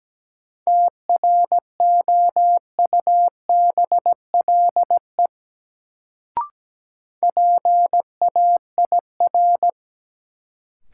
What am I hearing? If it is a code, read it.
Morse code, heard as a keyed tone, about 17 wpm, 700 Hz. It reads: TROUBLE PAIR
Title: Sentences from Top 1200 Words - New Word Drill Alt - 17wpm